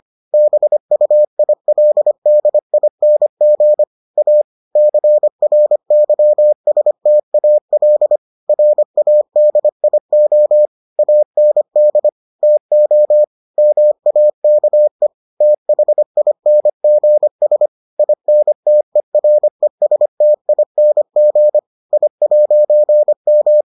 Morse code, heard as a keyed tone, about 25 wpm, 600 Hz. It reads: BUILDING A CRYSTAL RADIO AND TO MAKE THINGS INTERESTING I'M